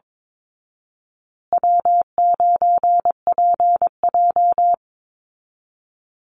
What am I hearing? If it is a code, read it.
Morse code, heard as a keyed tone, about 22 wpm, 700 Hz. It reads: W9PJ
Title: Easy Call Signs Repeated 3x - 22wpm